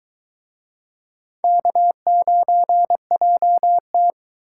Morse code, heard as a keyed tone, about 24 wpm, 700 Hz. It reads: K9JT